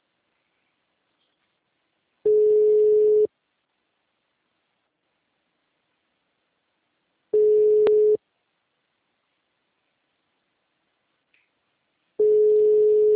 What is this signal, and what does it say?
Ringback tone (Europe, 425 Hz).